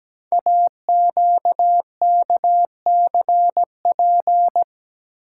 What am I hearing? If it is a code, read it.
Morse code, heard as a keyed tone, about 17 wpm, 700 Hz. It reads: AQKCP
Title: Code Groups - 5 Letters - 17wpm